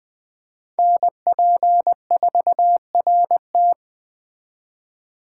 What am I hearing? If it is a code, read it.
Morse code, heard as a keyed tone, about 21 wpm, 700 Hz. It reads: NP4RT
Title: Speed Racing - USA Call Signs 17wpm